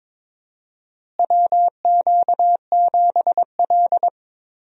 Morse code, heard as a keyed tone, about 22 wpm, 700 Hz. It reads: WQ7L
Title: USA Call Signs - 4 Characters - Repeated 2x - 22wpm